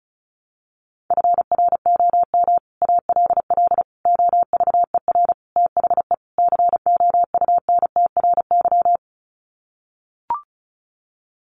Morse code, heard as a keyed tone, about 34 wpm, 700 Hz. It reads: FROM ALL OVER THE COUNTRY
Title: Sets of 5 Words Encore 35wpm